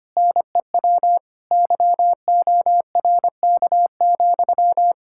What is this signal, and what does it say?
Morse code, heard as a keyed tone, about 25 wpm, 700 Hz. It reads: NEW YORK,
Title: K1USN SST QSO Practice 25wpm